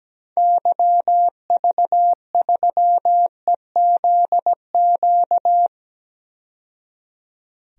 Morse code, heard as a keyed tone, about 17 wpm, 700 Hz. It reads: YV3EZQ
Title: DXCC-100 Call Signs - 6 Characters - 17wpm